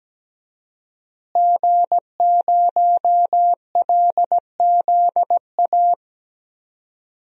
Morse code, heard as a keyed tone, about 17 wpm, 700 Hz. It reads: G0LZA
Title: DXCC-100 Call Signs - 5 Characters - Repeated 2x - 17wpm